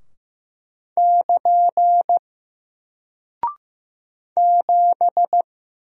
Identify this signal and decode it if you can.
Morse code, heard as a keyed tone, about 15 wpm, 700 Hz. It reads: <KN> 7
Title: Single Character (Letter:Number:Punctuation:Prosign) Alt 15wpm